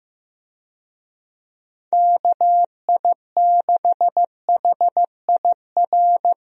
Morse code, heard as a keyed tone, about 15 wpm, 700 Hz. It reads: KI6HIR
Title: USA Call Signs - 6 Characters - Repeated 3x - 15wpm